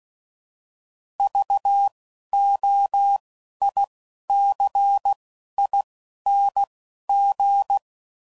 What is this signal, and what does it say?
Morse code, heard as a keyed tone, about 16 wpm, spaced as for 11 wpm, 800 Hz. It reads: VOICING
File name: Prefixes / By First Letter / Vs